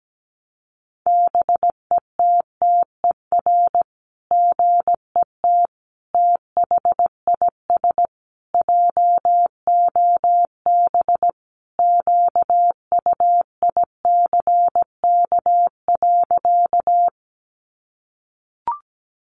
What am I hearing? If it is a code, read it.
Morse code, heard as a keyed tone, about 17 wpm, 700 Hz. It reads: BETTER GET THIS JOB QUICK.
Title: Sentences from Top 600 Words 17wpm